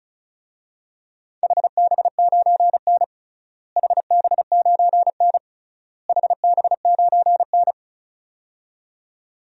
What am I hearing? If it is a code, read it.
Morse code, heard as a keyed tone, about 35 wpm, 700 Hz. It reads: HB9N HB9N HB9N
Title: DXCC-100 Call Signs - 4 Characters - Repeated 3x - 35wpm